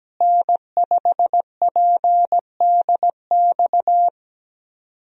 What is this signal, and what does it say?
Morse code, heard as a keyed tone, about 17 wpm, 700 Hz. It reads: N5PDX
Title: USA Call Signs - 5 Characters - Repeated 3x - 17wpm